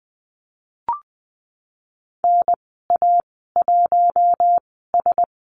Morse code, heard as a keyed tone, about 20 wpm, spaced as for 15 wpm, 700 Hz. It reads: NA1S
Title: 048 Lesson 9g - Number 4 - Call Signs 20wpm